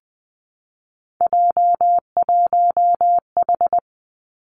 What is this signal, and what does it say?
Morse code, heard as a keyed tone, about 20 wpm, 700 Hz. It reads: J1H
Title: Three Challenging Letter-Number Combinations 20wpm